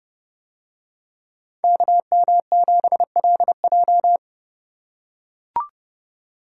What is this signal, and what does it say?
Morse code, heard as a keyed tone, about 28 wpm, 700 Hz. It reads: KM7LJ